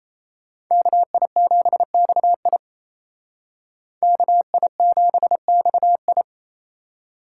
Morse code, heard as a keyed tone, about 36 wpm, 700 Hz. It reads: KI7XI KI7XI